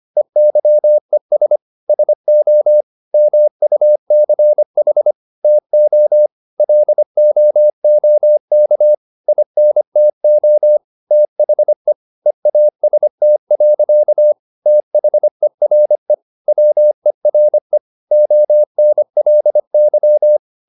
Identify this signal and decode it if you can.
Morse code, heard as a keyed tone, about 25 wpm, 600 Hz. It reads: EYES SO MUCH TO LOOK INTO THE EAST. THERE WERE ONLY
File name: the old man and the sea part 1 25wpm